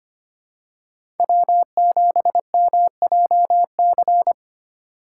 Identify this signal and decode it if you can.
Morse code, heard as a keyed tone, about 25 wpm, 700 Hz. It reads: W7MJC
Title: USA Call Signs - 2x Character Spacing - 25wpm